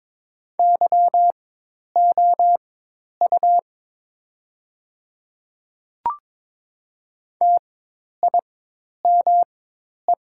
Morse code, heard as a keyed tone, about 22 wpm, spaced as for 10 wpm, 700 Hz. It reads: YOU TIME
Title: 4x Character Spacing - Top 300 Words - 22wpm